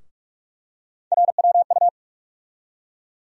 Morse code, heard as a keyed tone, about 46 wpm, 700 Hz. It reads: RWU
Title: USA Call Sign Postfixes - 3 Letters 45wpm